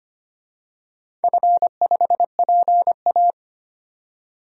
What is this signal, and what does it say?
Morse code, heard as a keyed tone, about 25 wpm, 700 Hz. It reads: F5PA
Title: DXCC-100 Call Signs - 4 Characters - 25wpm